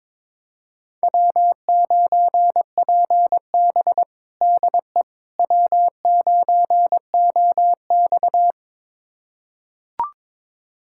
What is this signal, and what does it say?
Morse code, heard as a keyed tone, about 22 wpm, 700 Hz. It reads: W9PB DE W9OX
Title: USA Callsign DE Callsign - 4 Characters - 22wpm